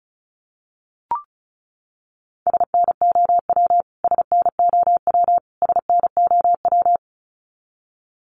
Morse code, heard as a keyed tone, about 34 wpm, 700 Hz. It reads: SNOW SNOW SNOW
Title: Top 500 Words Repeated 35wpm